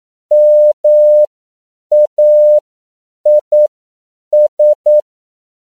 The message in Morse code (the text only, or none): MAIS